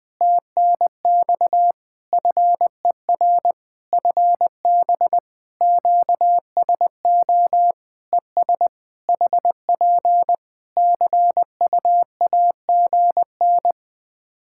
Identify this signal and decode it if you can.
Morse code, heard as a keyed tone, about 20 wpm, 700 Hz. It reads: TNX FER FB QSO ES HP CUAGN
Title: Practice QSO - Exchange 4 - Ending - 20wpm